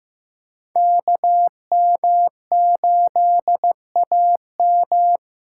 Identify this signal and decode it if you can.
Morse code, heard as a keyed tone, about 15 wpm, 700 Hz. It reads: KM8AM
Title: USA Callsign DE Callsign - 5 Characters - 15wpm